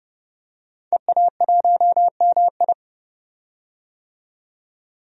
Morse code, heard as a keyed tone, about 30 wpm, 700 Hz. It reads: EA1MI
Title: DXCC-100 Call Signs - 5 Characters - 30wpm